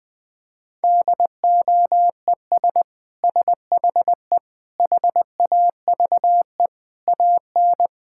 Morse code, heard as a keyed tone, about 21 wpm, 700 Hz. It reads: DOES SHE HAVE AN
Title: Sentences from Top 1000 Words - Review Part 6 - 20wpm